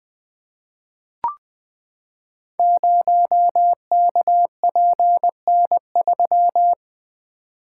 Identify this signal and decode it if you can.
Morse code, heard as a keyed tone, about 20 wpm, 700 Hz. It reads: 0KPN3
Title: Code Groups - 5 Letter:Numbers - 20wpm